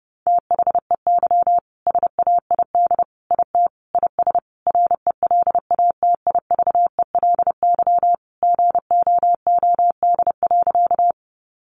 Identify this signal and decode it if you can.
Morse code, heard as a keyed tone, about 30 wpm, 700 Hz. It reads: THEY SAID IT IS RELATIVELY GOOD.
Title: Sentences from Top 1100 Words 30wpm